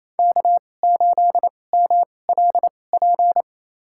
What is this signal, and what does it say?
Morse code, heard as a keyed tone, about 28 wpm, spaced as for 20 wpm, 700 Hz. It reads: K8MLP
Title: USA Call Signs - 5 Characters - 2x Character Spacing - 28wpm